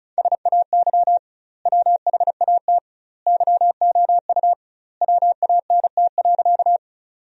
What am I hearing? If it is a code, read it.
Morse code, heard as a keyed tone, about 36 wpm, 700 Hz. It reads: SAY WHAT YOU WANT.
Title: 2x Word-Spacing - Sentences from Top 200 Words - 35wpm